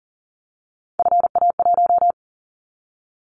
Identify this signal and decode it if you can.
Morse code, heard as a keyed tone, about 40 wpm, 700 Hz. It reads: FA1
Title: Callsign Prefixes for Top 50 DXCC - 40wpm